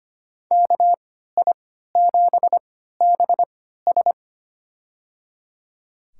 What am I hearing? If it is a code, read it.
Morse code, heard as a keyed tone, about 25 wpm, spaced as for 14 wpm, 700 Hz. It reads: KI7BS
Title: USA Call Signs - 5 Characters - 3x Character Spacing - 25wpm